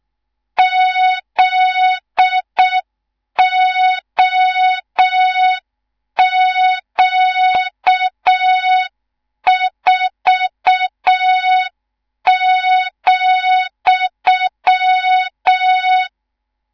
Morse code, heard as a keyed tone, about 6 wpm, 3700 Hz, with noise. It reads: ZOQ4,
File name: Morze-HA2MN / hang / 18 IRASJELEK2